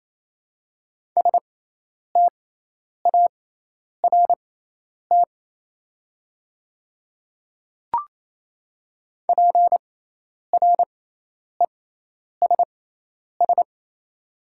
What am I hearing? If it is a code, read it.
Morse code, heard as a keyed tone, about 27 wpm, spaced as for 10 wpm, 700 Hz. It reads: START PRESS
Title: 6x Character Spacing - Top 300 Words - 28wpm